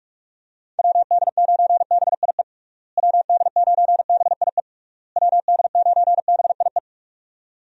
Morse code, heard as a keyed tone, about 46 wpm, 700 Hz. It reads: WD9BIE WD9BIE WD9BIE
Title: USA Call Signs - 6 Characters - Repeated 3x - 45wpm